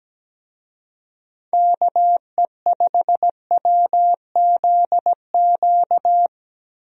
Morse code, heard as a keyed tone, about 17 wpm, 700 Hz. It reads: KE5WZQ
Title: USA Call Signs - 6 Characters - Repeated 2x - 17wpm